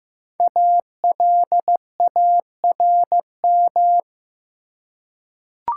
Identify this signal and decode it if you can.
Morse code, heard as a keyed tone, about 15 wpm, 700 Hz. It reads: ALARM
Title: Five Letter Words - 15wpm